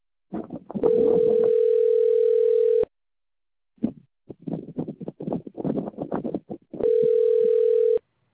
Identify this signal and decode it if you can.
Ringback tone (North America), 440 + 480 Hz.